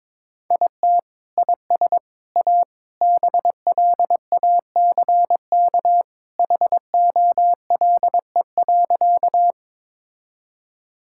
Morse code, heard as a keyed tone, about 22 wpm, 700 Hz. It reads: IT IS A BLACK HOLE.
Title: Sentences from Top 1600 Words 22wpm